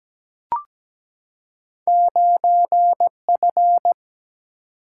Morse code, heard as a keyed tone, about 17 wpm, 700 Hz. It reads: 9F
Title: Single-Letter Single-Number 17wpm